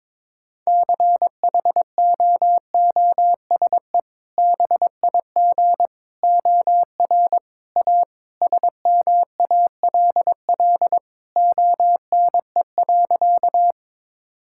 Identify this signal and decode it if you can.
Morse code, heard as a keyed tone, about 22 wpm, 700 Hz. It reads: CHOOSE BIG OR A SMALL ONE.
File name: practice / 22wpm / Sentences with 2-Word Phrases - 22wpm